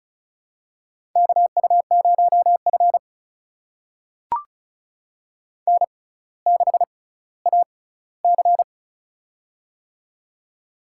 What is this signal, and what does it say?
Morse code, heard as a keyed tone, about 36 wpm, spaced as for 18 wpm, 700 Hz. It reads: KU0F N6AC
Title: USA Call Signs - 4 Characters - 6x Character Spacing - 35wpm